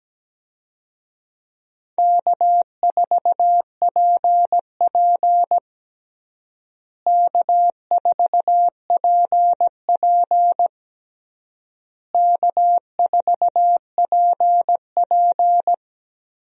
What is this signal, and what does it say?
Morse code, heard as a keyed tone, about 17 wpm, 700 Hz. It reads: K4PP K4PP K4PP
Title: USA Call Signs - 4 Characters - Repeated 3x - 17wpm